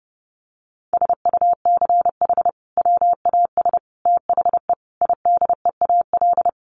Morse code, heard as a keyed tone, about 28 wpm, 700 Hz. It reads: SUCH WAS THE IDEAL
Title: Sentences from Top 1400 Words 30wpm